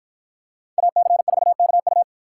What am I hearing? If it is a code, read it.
Morse code, heard as a keyed tone, about 48 wpm, 700 Hz. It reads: AC4KU